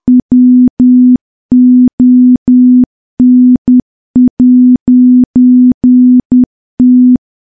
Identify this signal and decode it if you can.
Morse code, heard as a keyed tone, about 10 wpm, 250 Hz. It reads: WON'T
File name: morse / morse-training-10WPM